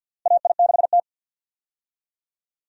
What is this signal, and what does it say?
Morse code, heard as a keyed tone, about 50 wpm, 700 Hz. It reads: AI6T
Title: USA Call Signs - 4 Characters - Repeated 3x - 50wpm